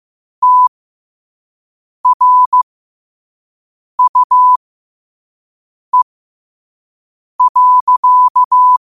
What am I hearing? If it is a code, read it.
Morse code, heard as a keyed tone, about 15 wpm, spaced as for 5 wpm, 1000 Hz. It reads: TRUE.
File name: qotdfiles / 5wpm